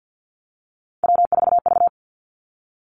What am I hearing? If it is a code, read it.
Morse code, heard as a keyed tone, about 50 wpm, 700 Hz. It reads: W4V
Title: Three Challenging Letter-Number Combinations 50wpm